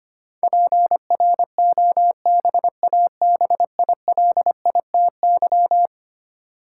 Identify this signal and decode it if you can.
Morse code, heard as a keyed tone, about 25 wpm, 700 Hz. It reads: PROBABILITY
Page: morse-code-ninja-practice-25wpm